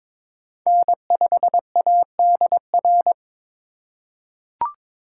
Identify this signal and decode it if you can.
Morse code, heard as a keyed tone, about 22 wpm, 700 Hz. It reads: N5ADR
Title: USA Call Signs - 5 Characters - 4x Character Spacing - 22wpm